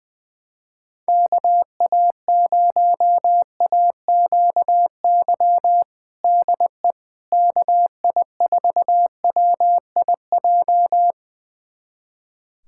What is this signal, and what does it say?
Morse code, heard as a keyed tone, about 21 wpm, 700 Hz. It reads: KA0AQY DE KI4WIJ
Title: USA Callsign DE Callsign - 6 Characters - 20wpm